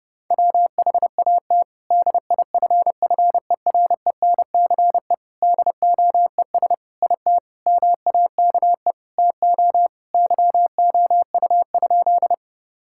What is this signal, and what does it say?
Morse code, heard as a keyed tone, about 30 wpm, 700 Hz. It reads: WHAT DIFFERENCE DOES IT MAKE TO YOU?
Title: Sentences from Top 1000 Words - Review Part 5 - 30wpm